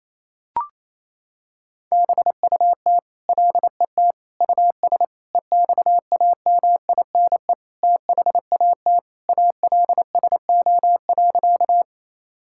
Morse code, heard as a keyed tone, about 28 wpm, 700 Hz. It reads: BUT LET US EXAMINE THAT ALSO.
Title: Sentences from Top 1600 Words 28wpm